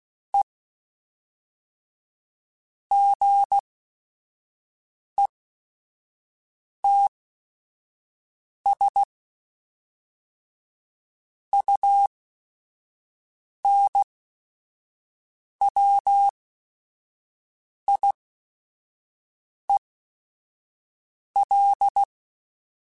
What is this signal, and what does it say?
Morse code, heard as a keyed tone, about 16 wpm, spaced as for 5 wpm, 800 Hz. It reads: E GETS UNWIEL
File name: Technical Subjects / Technical Subjects-mp3s / AND Gate @5